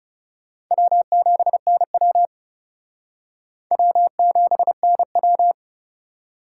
Morse code, heard as a keyed tone, about 37 wpm, 700 Hz. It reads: W7NW W7NW
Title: Speed Racing - USA 1x2 Call Signs 30wpm